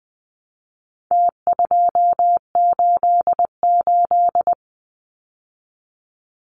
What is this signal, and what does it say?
Morse code, heard as a keyed tone, about 20 wpm, 700 Hz. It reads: T288
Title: Numbers - Contest Style - 4-Digit 20wpm